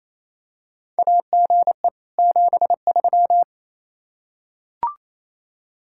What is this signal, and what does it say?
Morse code, heard as a keyed tone, about 28 wpm, 700 Hz. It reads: AGE 73
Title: Practice QSO - Exchange 3 - More Info - 28wpm